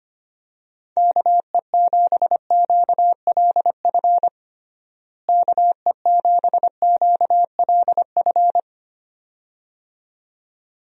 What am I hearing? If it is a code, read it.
Morse code, heard as a keyed tone, about 25 wpm, 700 Hz. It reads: KE7QLF KE7QLF
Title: USA Call Signs - 6 Characters - Repeated 2x - 25wpm